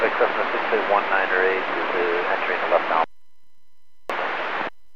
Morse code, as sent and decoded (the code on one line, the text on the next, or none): none